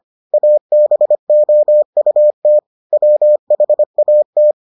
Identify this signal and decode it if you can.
Morse code, heard as a keyed tone, about 25 wpm, 600 Hz. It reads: ABOUT WHAT